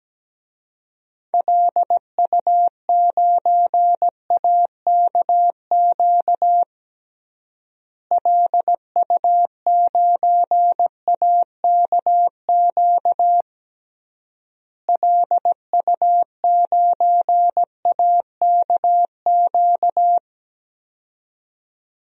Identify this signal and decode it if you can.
Morse code, heard as a keyed tone, about 17 wpm, 700 Hz. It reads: LU9AKQ LU9AKQ LU9AKQ
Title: DXCC-100 Call Signs - 6 Characters - Repeated 3x - 17wpm